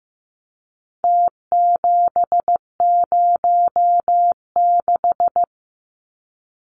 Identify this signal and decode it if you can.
Morse code, heard as a keyed tone, about 15 wpm, 700 Hz. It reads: T706
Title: Numbers - Contest Style - 4-Digit 15wpm